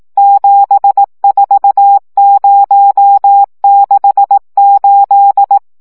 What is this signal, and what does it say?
Morse code, heard as a keyed tone, about 18 wpm, 800 Hz. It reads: 74068
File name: morze vizsga 2011-3 15